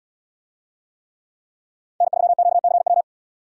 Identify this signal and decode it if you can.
Morse code, heard as a keyed tone, about 77 wpm, 700 Hz. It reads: N3QCW